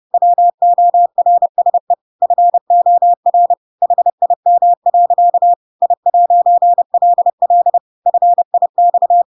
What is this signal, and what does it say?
Morse code, heard as a keyed tone, about 30 wpm, 700 Hz. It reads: WORSE FOR HIM. I'LL FIX
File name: Pinocchio / Pinocchio ch1 30 30 ik1hge